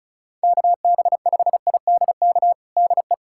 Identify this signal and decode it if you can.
Morse code, heard as a keyed tone, about 36 wpm, 700 Hz. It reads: KB5IDK DE
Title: USA Callsign DE Callsign - 6 Characters - 35wpm